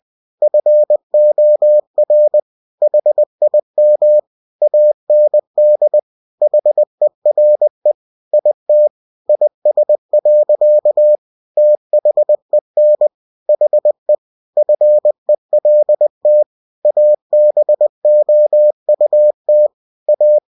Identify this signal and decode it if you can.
Morse code, heard as a keyed tone, about 20 wpm, 600 Hz. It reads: FOR HIM AND HERE IT IS. THEN HE FELT ABOUT A
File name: cw-books / winnie-the-pooh / winnie-20wpm-page-0107